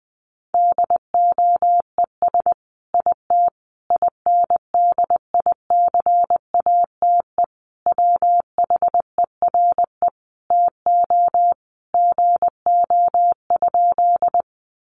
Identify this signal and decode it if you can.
Morse code, heard as a keyed tone, about 20 wpm, 700 Hz. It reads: DOES IT INDICATE WHERE TO GO?
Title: Sentences from Top 1300 Words 20wpm